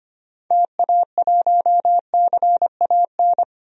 Morse code, heard as a keyed tone, about 25 wpm, 700 Hz. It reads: TA1CAN
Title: DXCC-100 Call Signs - 6 Characters - 25wpm